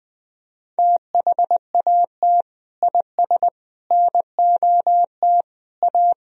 Morse code, heard as a keyed tone, about 20 wpm, 700 Hz. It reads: THAT IS NOT A
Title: Sentences from Top 1000 Words - Review Part 3 - 20wpm